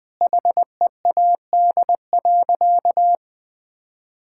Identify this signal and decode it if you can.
Morse code, heard as a keyed tone, about 20 wpm, 700 Hz. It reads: HEAD.
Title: Sentences from Top 1000 Words - Review Part 6 - 20wpm